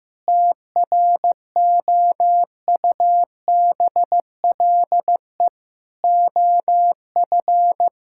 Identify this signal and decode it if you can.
Morse code, heard as a keyed tone, about 15 wpm, 700 Hz. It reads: TROUBLE OF
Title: Sentences from Top 1600 Words 15wpm